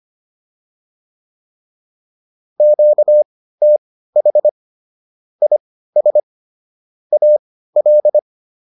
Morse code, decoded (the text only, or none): QTH IS AL